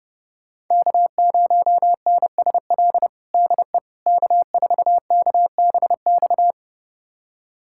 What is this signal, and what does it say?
Morse code, heard as a keyed tone, about 30 wpm, 700 Hz. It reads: K0NSL DE K4KBX
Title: USA Callsign DE Callsign - 5 Characters - 30wpm